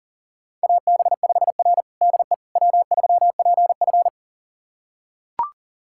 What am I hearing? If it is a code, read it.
Morse code, heard as a keyed tone, about 40 wpm, 700 Hz. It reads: AB5R DE W3PF
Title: USA Callsign DE Callsign - 4 Characters - 40wpm